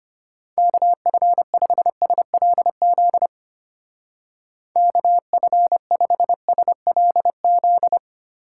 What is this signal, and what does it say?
Morse code, heard as a keyed tone, about 31 wpm, 700 Hz. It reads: KF5SLZ KF5SLZ